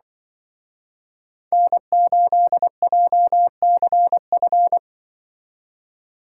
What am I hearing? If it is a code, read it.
Morse code, heard as a keyed tone, about 25 wpm, 700 Hz. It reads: N8JCF